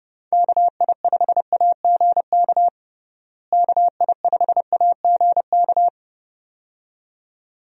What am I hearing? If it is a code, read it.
Morse code, heard as a keyed tone, about 30 wpm, 700 Hz. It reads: KI5AGK KI5AGK
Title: USA Call Signs - 6 Characters - Repeated 2x - 30wpm